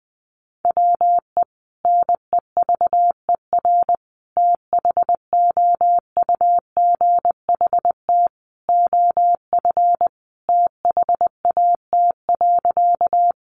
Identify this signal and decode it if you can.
Morse code, heard as a keyed tone, about 20 wpm, 700 Hz. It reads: WE NEVER THOUGHT OF THAT.
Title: Sentences from Top 300 Words 20wpm